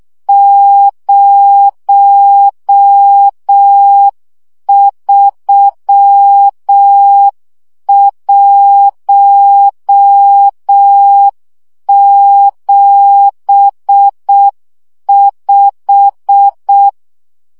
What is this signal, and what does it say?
Morse code, heard as a keyed tone, about 6 wpm, 800 Hz. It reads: 03175
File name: morze vizsga 2011-1 5